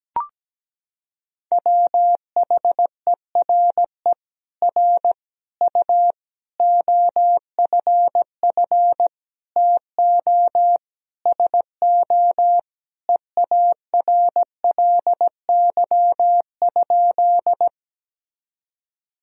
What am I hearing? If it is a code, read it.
Morse code, heard as a keyed tone, about 17 wpm, 700 Hz. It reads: WHERE R U OFF TO SO EARLY?
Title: Sentences from Top 400 Words and Abbreviations 17wpm